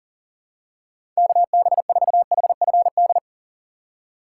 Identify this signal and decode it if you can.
Morse code, heard as a keyed tone, about 39 wpm, 700 Hz. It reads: KB4HFD
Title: USA Call Signs - 6 Characters - 4x Character Spacing - 40wpm